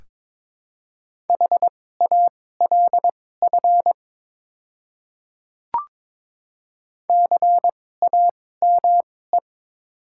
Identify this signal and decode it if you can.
Morse code, heard as a keyed tone, about 22 wpm, spaced as for 16 wpm, 700 Hz. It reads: HALF CAME
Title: 2x Character Spacing - Top 500 Words - 22wpm